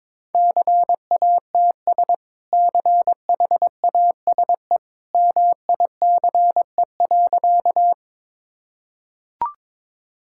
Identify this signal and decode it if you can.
Morse code, heard as a keyed tone, about 22 wpm, 700 Hz. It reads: CATS CHASE MICE.
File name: practice / 22wpm / Easy Sentences with Hints - 22wpm